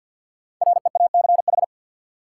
Morse code, heard as a keyed tone, about 51 wpm, 700 Hz. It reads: REACH
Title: Sentences from Top 400 Words - New Word Drill - 50wpm